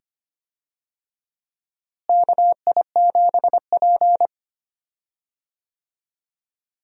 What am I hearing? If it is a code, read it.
Morse code, heard as a keyed tone, about 25 wpm, 700 Hz. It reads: KI7P
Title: USA Call Signs - 4 Characters - Repeated 2x - 25wpm